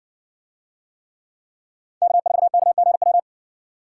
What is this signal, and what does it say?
Morse code, heard as a keyed tone, about 60 wpm, 700 Hz. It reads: K4KGW